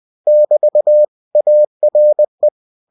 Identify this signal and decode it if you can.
Morse code, heard as a keyed tone, about 20 wpm, 600 Hz. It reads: <BT> ARE
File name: cw-books / winnie-the-pooh / winnie-20wpm-page-0105